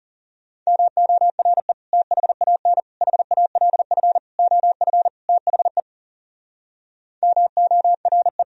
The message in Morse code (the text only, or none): MORE THAN HALF OF THE MORE